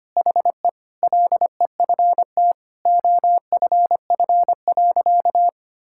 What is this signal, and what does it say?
Morse code, heard as a keyed tone, about 25 wpm, 700 Hz. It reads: HE LEFT OFF.
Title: Sentences from Top 1000 Words - Review Part 6 - 25wpm